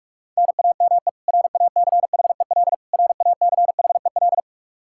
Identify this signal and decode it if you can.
Morse code, heard as a keyed tone, about 41 wpm, 700 Hz. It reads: NAME RACHEL RACHEL
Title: Practice QSO - Exchange 1 - Beginning - 45wpm